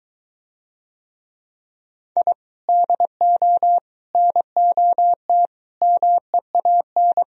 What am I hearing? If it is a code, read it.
Morse code, heard as a keyed tone, about 23 wpm, 700 Hz. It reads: I DO NOT MEAN